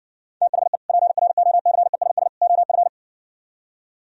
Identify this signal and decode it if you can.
Morse code, heard as a keyed tone, about 59 wpm, 700 Hz. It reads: THE PROCESS OF